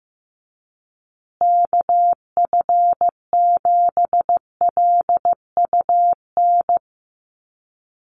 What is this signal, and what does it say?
Morse code, heard as a keyed tone, about 15 wpm, 700 Hz. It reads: KF7LUN
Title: USA Call Signs Repeated 3x 15wpm